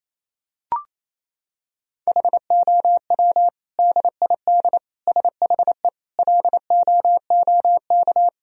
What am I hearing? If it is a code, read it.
Morse code, heard as a keyed tone, about 28 wpm, 700 Hz. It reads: HOW DID SHE LOOK
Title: Sentences from Top 1000 Words - Review Part 3 - 28wpm